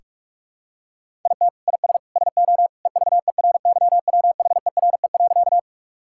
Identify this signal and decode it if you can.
Morse code, heard as a keyed tone, about 46 wpm, 700 Hz. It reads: IT IS SO EVERYWHERE.
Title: Sentences from Top 1600 Words 45wpm